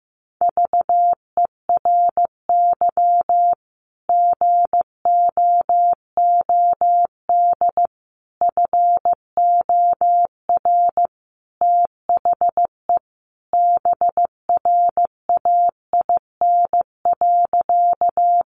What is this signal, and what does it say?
Morse code, heard as a keyed tone, about 15 wpm, 700 Hz. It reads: VERY GOOD FOR THE BRAIN.
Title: Sentences from Top 1100 Words 15wpm